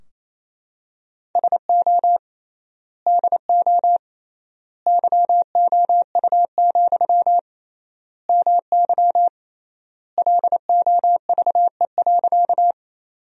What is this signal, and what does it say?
Morse code, heard as a keyed tone, about 28 wpm, 700 Hz. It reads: SO DO YOU, MY LOVE.
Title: 3x Word-Spacing - Sentences from Top 400 Words - 28wpm